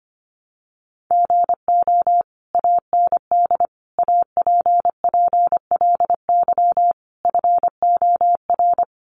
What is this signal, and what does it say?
Morse code, heard as a keyed tone, about 25 wpm, 700 Hz. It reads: GO AND APPLY FOR
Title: Sentences from Top 1200 Words 25wpm